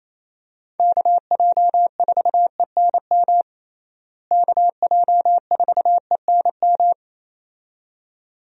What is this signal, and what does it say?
Morse code, heard as a keyed tone, about 28 wpm, 700 Hz. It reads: KJ4ENM KJ4ENM
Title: USA Call Signs - 6 Characters - Repeated 2x - 28wpm